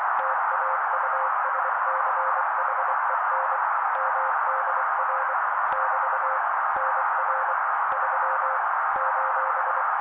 Morse code, heard as a keyed tone, about 24 wpm, 530 Hz. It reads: TAUSCHEN MDR <BT> NR 3 :
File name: daily / 23wpm noise